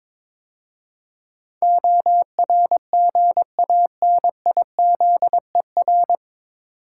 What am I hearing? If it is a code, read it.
Morse code, heard as a keyed tone, about 22 wpm, 700 Hz. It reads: ORGANIZER
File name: 9 Letter Words - First 3 Letters Given - 22wpm